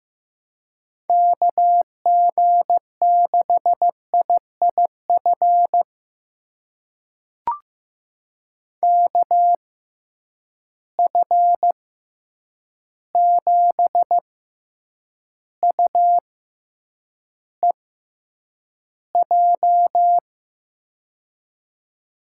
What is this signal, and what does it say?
Morse code, heard as a keyed tone, about 15 wpm, spaced as for 8 wpm, 700 Hz. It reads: KG6IIF KF7UEJ
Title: USA Call Signs - 6 Characters - 6x Character Spacing - 15wpm